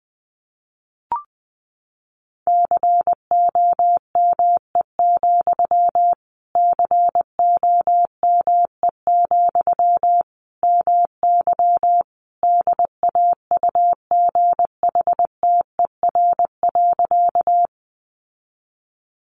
Morse code, heard as a keyed tone, about 19 wpm, 700 Hz. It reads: COME, COME, MY DAUGHTER.
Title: Sentences from Top 1500 Words 20wpm